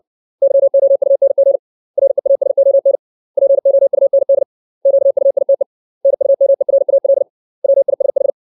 Morse code, heard as a keyed tone, about 60 wpm, 550 Hz. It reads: YOUNG REASON POUND QUITE NUMERAL WISH
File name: 632 words / 632 Words 60WPM